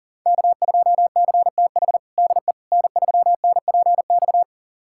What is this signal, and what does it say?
Morse code, heard as a keyed tone, about 40 wpm, 700 Hz. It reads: K2CTH DE N3NPX
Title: USA Callsign DE Callsign - 5 Characters - 40wpm